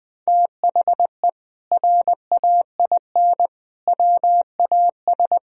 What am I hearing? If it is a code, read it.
Morse code, heard as a keyed tone, about 20 wpm, 700 Hz. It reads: THE RAIN WAS
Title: Sentences from Top 1000 Words - Review Part 1 - 20wpm